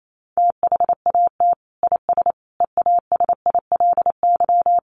Morse code, heard as a keyed tone, about 28 wpm, 700 Hz. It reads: THAT IS EASILY
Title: Sentences from Top 900 Words 28wpm